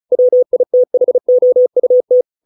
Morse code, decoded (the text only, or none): WITHOUT